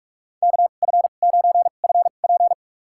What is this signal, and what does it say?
Morse code, heard as a keyed tone, about 46 wpm, spaced as for 33 wpm, 700 Hz. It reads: KF9FP
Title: USA Call Signs - 5 Characters - 2x Character Spacing - 45wpm